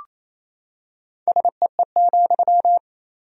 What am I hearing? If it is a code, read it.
Morse code, heard as a keyed tone, about 28 wpm, 700 Hz. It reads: SEE,
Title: 3x Word-Spacing - Sentences from Top 200 Words - 28wpm